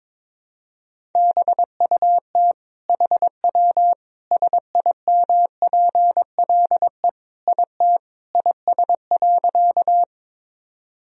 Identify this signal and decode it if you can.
Morse code, heard as a keyed tone, about 22 wpm, 700 Hz. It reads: BUT HW SIMPLE IT IS.
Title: Sentences from Top 400 Words and Abbreviations 22wpm